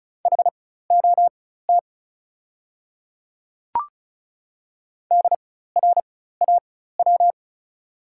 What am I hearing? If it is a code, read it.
Morse code, heard as a keyed tone, about 34 wpm, spaced as for 16 wpm, 690 Hz. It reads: HOT DRAW
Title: 4x Character Spacing - Top 500 Words - 35wpm